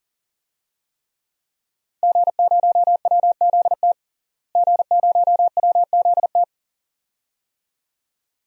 Morse code, heard as a keyed tone, about 38 wpm, 700 Hz. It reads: G0WZT G0WZT
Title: DXCC-100 Call Signs - 5 Characters - Repeated 2x - 40wpm